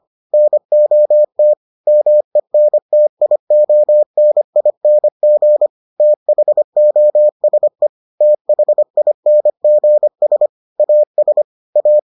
Morse code, heard as a keyed tone, about 25 wpm, 600 Hz. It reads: NOT MENTIONING THOSE THINGS AS A